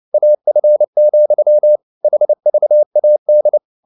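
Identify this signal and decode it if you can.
Morse code, heard as a keyed tone, about 29 wpm, 600 Hz. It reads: AF, HVAD
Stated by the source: Fast (QRQ) Morse at 29 wpm, from CW med Gnister.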